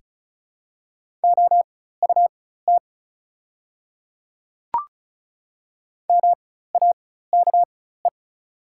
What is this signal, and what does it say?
Morse code, heard as a keyed tone, about 36 wpm, spaced as for 16 wpm, 700 Hz. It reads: OUT MAKE
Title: 4x Character Spacing - Top 100 Words - 35wpm